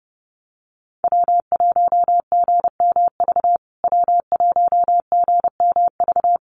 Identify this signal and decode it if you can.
Morse code, heard as a keyed tone, about 30 wpm, 700 Hz. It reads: W1GMV W1GMV
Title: USA Call Signs Repeated 3x 30wpm